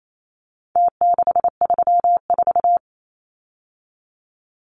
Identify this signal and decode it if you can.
Morse code, heard as a keyed tone, about 28 wpm, 700 Hz. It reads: T634
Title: Numbers - Contest Style - 4-Digit 28wpm